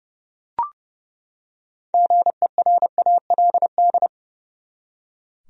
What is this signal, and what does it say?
Morse code, heard as a keyed tone, about 29 wpm, 700 Hz. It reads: GERALD
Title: USA Common Names - 30wpm